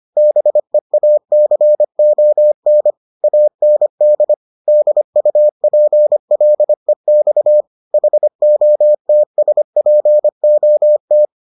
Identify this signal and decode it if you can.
Morse code, heard as a keyed tone, about 25 wpm, 600 Hz. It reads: BEACON AND DUPLEX HOTSPOT